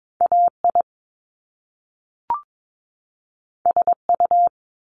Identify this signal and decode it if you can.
Morse code, heard as a keyed tone, about 22 wpm, 700 Hz. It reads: AI SU
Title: Top Two Letter Combinations in Words 22wpm